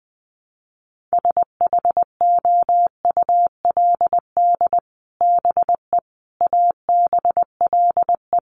Morse code, heard as a keyed tone, about 20 wpm, 700 Hz. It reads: SHOULD BE ABLE
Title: Sets of 4 Words 20wpm